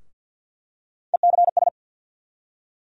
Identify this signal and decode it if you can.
Morse code, heard as a keyed tone, about 49 wpm, 700 Hz. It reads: ECS
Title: USA Call Sign Postfixes - 3 Letters 50wpm